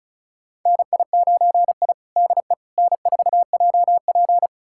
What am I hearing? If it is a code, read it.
Morse code, heard as a keyed tone, about 36 wpm, 700 Hz. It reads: NI9I DE N4JP
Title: USA Callsign DE Callsign - 4 Characters - 35wpm